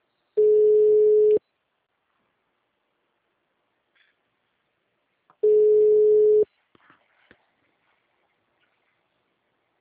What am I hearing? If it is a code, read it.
Ringback tone (Europe, 425 Hz).
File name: out- -2011- - - .61629